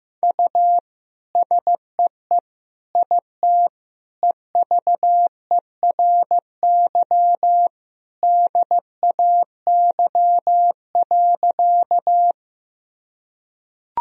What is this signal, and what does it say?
Morse code, heard as a keyed tone, about 16 wpm, 700 Hz. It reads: U SEE IT EVERY DAY.
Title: Sentences from Top 200 Words and Abbreviations 15wpm